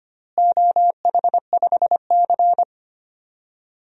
Morse code, heard as a keyed tone, about 25 wpm, 700 Hz. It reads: OH5C
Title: DXCC-100 Call Signs - 4 Characters - Repeated 3x - 25wpm